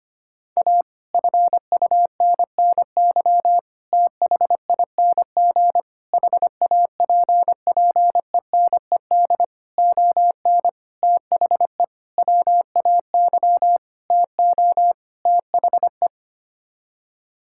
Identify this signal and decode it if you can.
Morse code, heard as a keyed tone, about 25 wpm, 700 Hz. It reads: A FUNNY THING HAPPENED ON THE WAY TO THE
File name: Sets of 9 Words - 25wpm